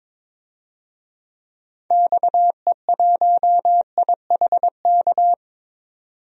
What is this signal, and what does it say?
Morse code, heard as a keyed tone, about 22 wpm, 700 Hz. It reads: XE1IHK
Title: DXCC-100 Call Signs - 6 Characters - 22wpm